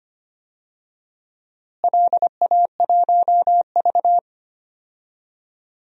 Morse code, heard as a keyed tone, about 25 wpm, 700 Hz. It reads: LA1V
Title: DXCC-100 Call Signs - 4 Characters - Repeated 2x - 25wpm